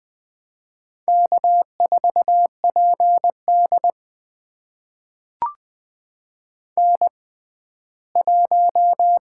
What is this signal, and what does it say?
Morse code, heard as a keyed tone, about 21 wpm, spaced as for 14 wpm, 700 Hz. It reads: K4PD N1
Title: USA Call Signs - 6x Character Spacing - 20wpm